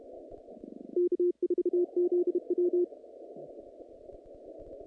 Morse code, heard as a keyed tone, about 32 wpm, 350 Hz. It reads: K4ZW
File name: contests / naqp-cw-k5tr-op-k5pi- -2300